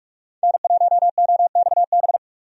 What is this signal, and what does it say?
Morse code, heard as a keyed tone, about 46 wpm, 700 Hz. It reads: N1OXB
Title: USA Call Signs - 5 Characters - 45wpm